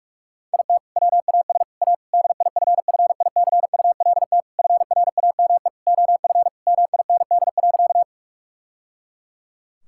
Morse code, heard as a keyed tone, about 46 wpm, 700 Hz. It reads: IT WAS A DIFFICULT FRAME OF MIND.
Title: Sentences from Top 1000 Words - Review Part 6 - 45wpm